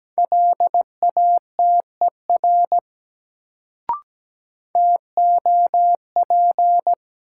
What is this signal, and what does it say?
Morse code, heard as a keyed tone, about 17 wpm, 700 Hz. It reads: LATER TOP
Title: Sentences from Top 700 Words - New Word Drill - 17wpm